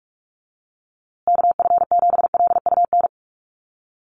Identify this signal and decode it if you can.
Morse code, heard as a keyed tone, about 46 wpm, 700 Hz. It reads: KF7LUN
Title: USA Call Signs Repeated 3x 45wpm